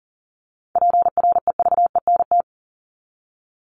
Morse code, heard as a keyed tone, about 40 wpm, 700 Hz. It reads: PREVENT
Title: Top 1001-1250 Words 40wpm